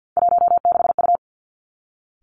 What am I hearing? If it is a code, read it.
Morse code, heard as a keyed tone, about 50 wpm, 700 Hz. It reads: 16U
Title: Three Challenging Letter-Number Combinations 50wpm